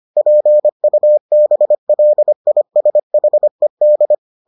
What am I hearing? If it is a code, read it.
Morse code, heard as a keyed tone, about 25 wpm, 600 Hz. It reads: PUBLISHED